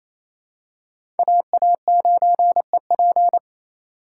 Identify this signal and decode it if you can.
Morse code, heard as a keyed tone, about 28 wpm, 700 Hz. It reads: AA9EP
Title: USA Call Signs - 5 Characters - 6x Character Spacing - 28wpm